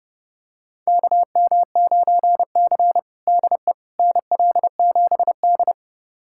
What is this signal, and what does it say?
Morse code, heard as a keyed tone, about 30 wpm, 700 Hz. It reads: KM9C DE NL7D
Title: USA Callsign DE Callsign - 4 Characters - 30wpm